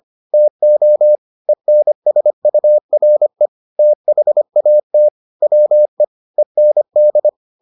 Morse code, heard as a keyed tone, about 25 wpm, 600 Hz. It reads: TO ENSURE THAT WE END